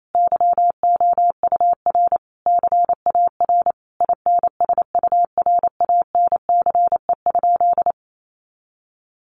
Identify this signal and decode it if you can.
Morse code, heard as a keyed tone, about 28 wpm, 700 Hz. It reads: YOUR CAR INSURANCE?
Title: Sentences from Top 1100 Words 28wpm